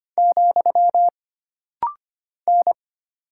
Morse code, heard as a keyed tone, about 25 wpm, 700 Hz. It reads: , N
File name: Single Character (Letter:Number:Punctuation:Prosign) - Rapid-Fire - 25wpm